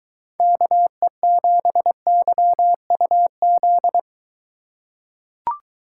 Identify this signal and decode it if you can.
Morse code, heard as a keyed tone, about 21 wpm, 700 Hz. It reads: KE7YUZ